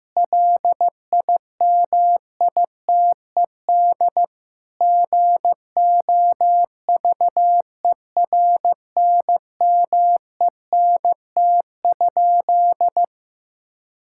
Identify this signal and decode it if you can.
Morse code, heard as a keyed tone, about 15 wpm, 700 Hz. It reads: LIMITED GOVERNMENT?
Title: Sentences from Top 1000 Words - Review Part 6 - 15wpm